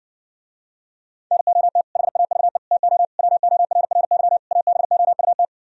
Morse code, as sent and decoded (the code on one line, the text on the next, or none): -. --- - / .... .- ...- . / - --- / .-- --- .-. .-. -.-- / .- -... --- ..- -
NOT HAVE TO WORRY ABOUT